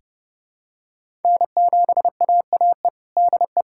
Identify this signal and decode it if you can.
Morse code, heard as a keyed tone, about 30 wpm, 700 Hz. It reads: N7AAE DE
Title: USA Callsign DE Callsign 30wpm